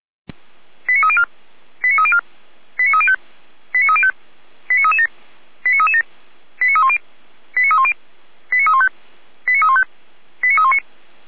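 Five-tone selective call (ZVEI), heard as quick runs of five tones, tones of about 70 ms. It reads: 89304 89304 89305 89305 89337 89337 89310 89310 89315 89315 89319
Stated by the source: SDR-received ZVEI five-tone selective call, code 89208.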